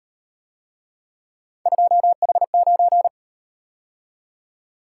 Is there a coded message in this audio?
Yes: Morse code from 1.7 to 3.1 s, heard as a keyed tone, about 38 wpm, 700 Hz. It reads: 2H9